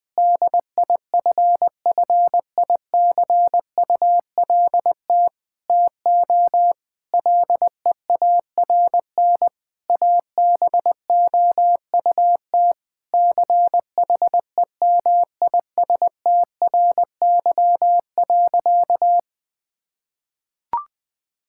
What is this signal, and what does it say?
Morse code, heard as a keyed tone, about 20 wpm, 700 Hz. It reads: DIFFICULT TO LEARN ABOUT CHEMISTRY.
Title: Sentences from Top 1000 Words - Review Part 6 - 20wpm